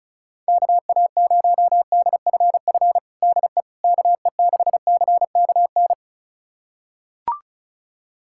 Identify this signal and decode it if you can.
Morse code, heard as a keyed tone, about 36 wpm, 700 Hz. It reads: KA0DFF DE KE6CKN
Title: USA Callsign DE Callsign 35wpm